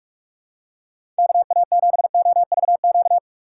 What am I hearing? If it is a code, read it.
Morse code, heard as a keyed tone, about 46 wpm, 700 Hz. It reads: KA7OVQ
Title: USA Call Signs - 6 Characters - 3x Character Spacing - 45wpm